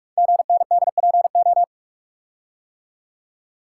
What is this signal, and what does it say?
Morse code, heard as a keyed tone, about 44 wpm, 690 Hz. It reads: GNDPO
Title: Code Groups - 5 Letters - 45wpm